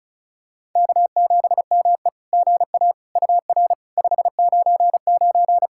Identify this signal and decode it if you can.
Morse code, heard as a keyed tone, about 36 wpm, 690 Hz. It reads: K7ME GA UR 599
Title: SOTA QSO Practice 35wpm